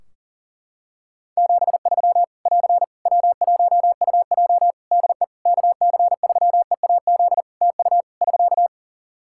Morse code, heard as a keyed tone, about 40 wpm, 700 Hz. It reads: 73 <AR> W1UJ DE KC3EAZ TU <SK>
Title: Practice QSO - Exchange 4 - Ending - 40wpm